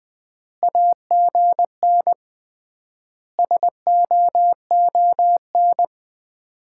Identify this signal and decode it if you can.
Morse code, heard as a keyed tone, about 21 wpm, 700 Hz. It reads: AGN SOON
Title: Common QSO Phrases - 3x Word-Spacing - 20wpm